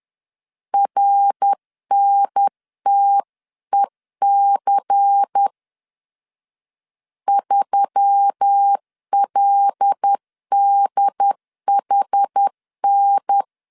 Morse code, heard as a keyed tone, about 10 wpm, 800 Hz. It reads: RNTEC 3LDHN
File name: CW / DET / 1 6 1500C S C 10wpm 4252